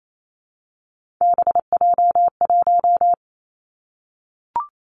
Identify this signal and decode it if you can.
Morse code, heard as a keyed tone, about 28 wpm, 700 Hz. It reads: BJ1
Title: Three Challenging Letter-Number Combinations 28wpm